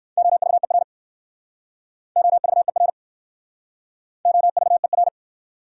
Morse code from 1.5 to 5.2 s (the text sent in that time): OVER OVER